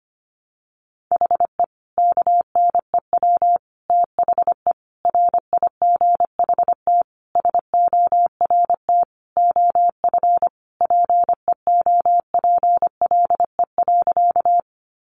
Morse code, heard as a keyed tone, about 25 wpm, 700 Hz. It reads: HE KNEW THE RIGHT SORT OF PEOPLE.
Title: Sentences from Top 1300 Words 25wpm